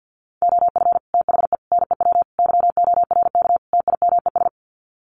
Morse code, heard as a keyed tone, about 50 wpm, 700 Hz. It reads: OF THE NEW YORK TIMES
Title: Sets of 5 Words 50wpm